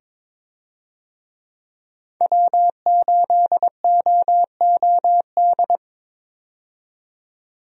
Morse code, heard as a keyed tone, about 21 wpm, 700 Hz. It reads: W8OOD
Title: USA Call Signs - 5 Characters - 22wpm